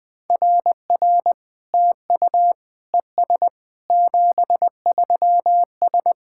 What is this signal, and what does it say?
Morse code, heard as a keyed tone, about 20 wpm, 700 Hz. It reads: RR TU ES 73S
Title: SOTA QSO Practice 20wpm